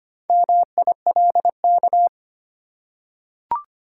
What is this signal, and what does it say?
Morse code, heard as a keyed tone, about 25 wpm, 690 Hz. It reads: MILK
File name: practice / 25wpm / Easy Words with Hints - 25wpm